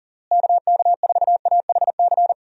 Morse code, heard as a keyed tone, about 40 wpm, 700 Hz. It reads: KK4AHC
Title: USA Call Signs - 6 Characters - Repeated 3x - 40wpm